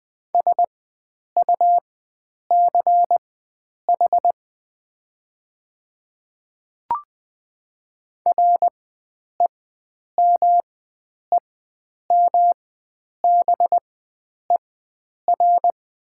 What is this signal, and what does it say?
Morse code, heard as a keyed tone, about 20 wpm, spaced as for 9 wpm, 700 Hz. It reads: SUCH REMEMBER
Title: 4x Character Spacing - Top 500 Words - 20wpm